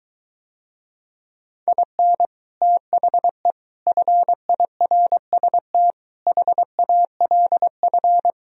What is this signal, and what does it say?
Morse code, heard as a keyed tone, about 23 wpm, 700 Hz. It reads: IN THE FIRST HALF